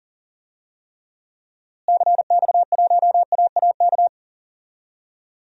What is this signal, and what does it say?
Morse code, heard as a keyed tone, about 40 wpm, 700 Hz. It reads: CX1AAK
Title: DXCC-100 Call Signs - 6 Characters - Repeated 2x - 40wpm